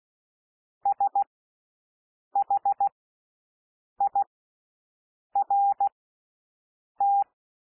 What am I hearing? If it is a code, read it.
Morse code, heard as a keyed tone, about 16 wpm, spaced as for 6 wpm, 800 Hz. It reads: SHIRT
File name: SOUND LETTERS 16 wpm